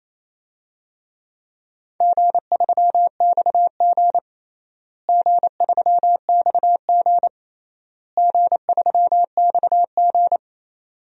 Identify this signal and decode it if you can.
Morse code, heard as a keyed tone, about 27 wpm, 700 Hz. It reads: G3XG G3XG G3XG
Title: DXCC-100 Call Signs - 4 Characters - Repeated 3x - 28wpm